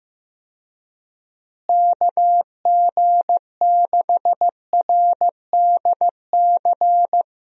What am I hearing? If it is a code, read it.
Morse code, heard as a keyed tone, about 15 wpm, 700 Hz. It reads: KG6RDC
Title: USA Call Signs - 6 Characters - Repeated 2x - 15wpm